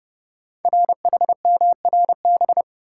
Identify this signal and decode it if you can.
Morse code, heard as a keyed tone, about 28 wpm, 700 Hz. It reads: RHMRB
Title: Code Groups - 5 Letters - 30wpm